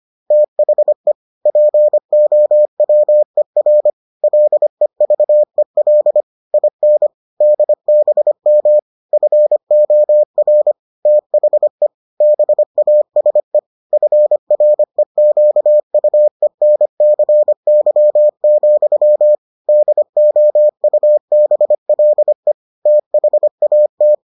Morse code, heard as a keyed tone, about 25 wpm, 600 Hz. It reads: THE POWER LEVEL IN DBM FOR THE BASE FREQUENCY, DOUBLE THAT